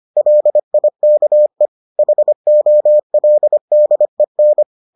Morse code, heard as a keyed tone, about 25 wpm, 600 Hz. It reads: LIKE HOLDEN